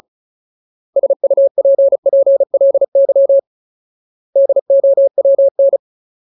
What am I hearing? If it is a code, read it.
Morse code, heard as a keyed tone, about 34 wpm, 550 Hz. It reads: SUPPLY DOWN
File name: Ultimate Word List / 35WPM Ultimate